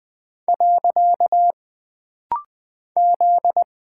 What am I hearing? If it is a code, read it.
Morse code, heard as a keyed tone, about 20 wpm, 700 Hz. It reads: . Z
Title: Single Character (Letter:Number:Punctuation:Prosign) - Rapid-Fire - 20wpm